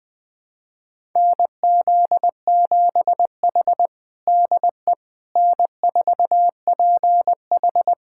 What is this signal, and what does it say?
Morse code, heard as a keyed tone, about 20 wpm, 700 Hz. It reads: NZ7H DE N4PH
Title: USA Callsign DE Callsign - 4 Characters - 20wpm